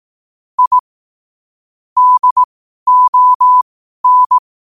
Morse code, heard as a keyed tone, about 18 wpm, spaced as for 12 wpm, 1000 Hz. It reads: I DON
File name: qotdfiles / 12wpm